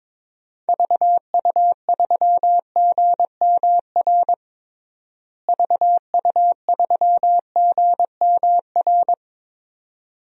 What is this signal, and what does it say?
Morse code, heard as a keyed tone, about 22 wpm, 700 Hz. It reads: VU3GMR VU3GMR
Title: DXCC-100 Call Signs - 6 Characters - Repeated 2x - 22wpm